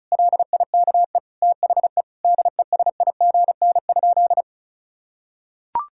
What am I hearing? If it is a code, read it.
Morse code, heard as a keyed tone, about 36 wpm, 700 Hz. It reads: LIKE THE DESIGN?
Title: Sentences from Top 1000 Words - Review Part 5 - 35wpm